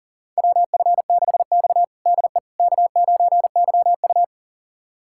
Morse code, heard as a keyed tone, about 40 wpm, 700 Hz. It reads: WF6X DE K9YU
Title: USA Callsign DE Callsign - 4 Characters - 40wpm